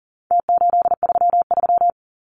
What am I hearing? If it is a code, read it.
Morse code, heard as a keyed tone, about 40 wpm, 700 Hz. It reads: T833
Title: Numbers - Contest Style - 4-Digit 40wpm